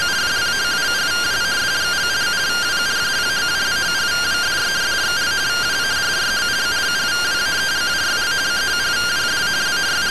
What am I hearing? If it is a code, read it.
Two-tone teleprinter signal, about 170 Hz shift.